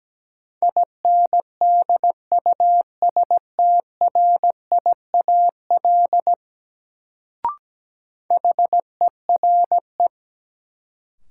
Morse code, heard as a keyed tone, about 17 wpm, 700 Hz. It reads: INDUSTRIAL HERE
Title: Sets of 5 Words - New Word Drill - 17wpm